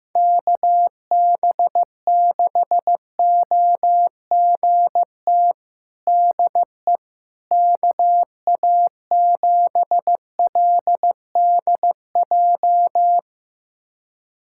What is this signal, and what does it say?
Morse code, heard as a keyed tone, about 15 wpm, 700 Hz. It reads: KB6OGT DE KA7LDJ
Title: USA Callsign DE Callsign - 6 Characters - 15wpm